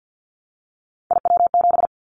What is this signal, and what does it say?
Morse code, heard as a keyed tone, about 50 wpm, 700 Hz. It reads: IW7
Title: Callsign Prefixes for Top 50 DXCC - 50wpm